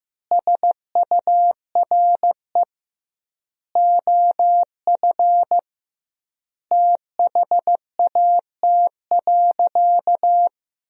Morse code, heard as a keyed tone, about 15 wpm, 700 Hz. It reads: SURE OF THAT.
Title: 2x Word-Spacing - Sentences from Top 300 Words - 15wpm